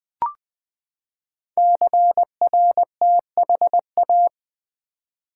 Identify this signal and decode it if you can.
Morse code, heard as a keyed tone, about 20 wpm, 700 Hz. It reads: CRTHA
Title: Code Groups - 5 Letters - 20wpm